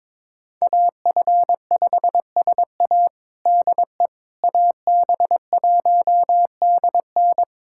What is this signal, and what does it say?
Morse code, heard as a keyed tone, about 22 wpm, 700 Hz. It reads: AF5SA DE AB1DN
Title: USA Callsign DE Callsign - 5 Characters - 22wpm